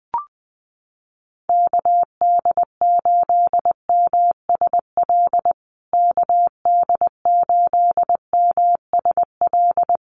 Morse code, heard as a keyed tone, about 20 wpm, 700 Hz. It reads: KD8MSL KD8MSL
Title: USA Call Signs Repeated 3x 20wpm